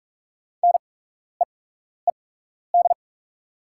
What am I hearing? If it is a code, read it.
Morse code, heard as a keyed tone, about 44 wpm, spaced as for 12 wpm, 690 Hz. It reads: NEED
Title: 8x Character Spacing - Top 300 Words - 45wpm